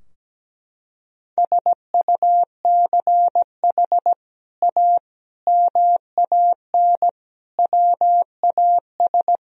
Morse code, heard as a keyed tone, about 17 wpm, 700 Hz. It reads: SUCH A MAN WAS
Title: Sentences from Top 1000 Words - Review Part 4 - 17wpm